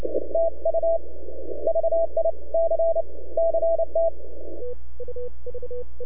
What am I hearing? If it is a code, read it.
Morse code, heard as a keyed tone, about 29 wpm, 660 Hz. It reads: TU VIC CT
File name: NAQPCW-8-2009 / W1UJ NAQP CW 09 1930Z